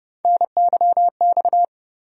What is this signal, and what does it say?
Morse code, heard as a keyed tone, about 30 wpm, 700 Hz. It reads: NYX
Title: USA Call Sign Postfixes - 3 Letters 30wpm